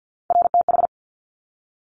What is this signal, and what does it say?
Morse code, heard as a keyed tone, about 50 wpm, 700 Hz. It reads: RTH
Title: Top Three Letter Combinations in Words 50wpm